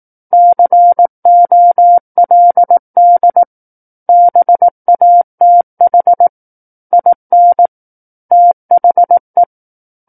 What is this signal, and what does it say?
Morse code, heard as a keyed tone, about 18 wpm, 700 Hz. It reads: COLD BATH IN THE
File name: 18 WPM